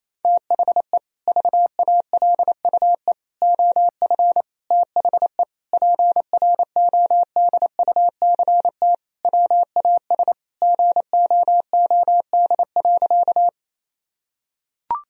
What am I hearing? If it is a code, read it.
Morse code, heard as a keyed tone, about 28 wpm, 700 Hz. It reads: THE VALUE OF THE PRODUCT WAS GOOD.
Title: Sentences from Top 1000 Words - Review Part 3 - 28wpm